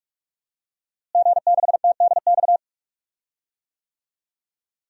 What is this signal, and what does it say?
Morse code, heard as a keyed tone, about 42 wpm, 700 Hz. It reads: G6TDX
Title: DXCC-100 Call Signs - 5 Characters - Repeated 2x - 45wpm